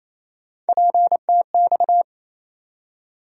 Morse code, heard as a keyed tone, about 28 wpm, 700 Hz. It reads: PTX
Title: USA Call Sign Postfixes - 3 Letters 28wpm